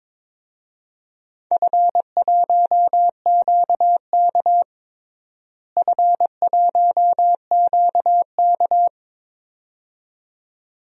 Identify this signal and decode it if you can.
Morse code, heard as a keyed tone, about 22 wpm, 700 Hz. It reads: F1QK F1QK
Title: DXCC-100 Call Signs - 4 Characters - Repeated 2x - 22wpm